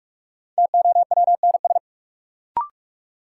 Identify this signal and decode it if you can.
Morse code, heard as a keyed tone, about 42 wpm, 700 Hz. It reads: TOWNS
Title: Sentences from Top 1000 Words - Review Part 6 - New Word Drill - 45wpm